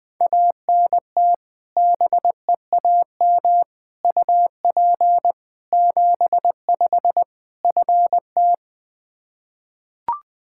Morse code, heard as a keyed tone, about 20 wpm, 700 Hz. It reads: ANT BEAM UP 75 FT
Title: Practice QSO - Exchange 2 - Info - 20wpm